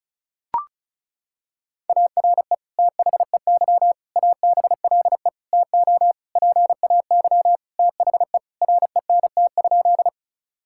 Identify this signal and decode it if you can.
Morse code, heard as a keyed tone, about 34 wpm, 700 Hz. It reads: ARE THEY ABLE TO PAY THE RENT?
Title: Sentences from Top 1000 Words - Review Part 5 - 35wpm